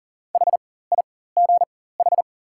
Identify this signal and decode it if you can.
Morse code, heard as a keyed tone, about 40 wpm, spaced as for 19 wpm, 690 Hz. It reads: HIGH
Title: 4x Character Spacing - Top 500 Words - 40wpm